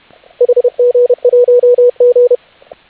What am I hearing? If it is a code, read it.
Morse code, heard as a keyed tone, about 31 wpm, 490 Hz, with radio noise. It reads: HG1G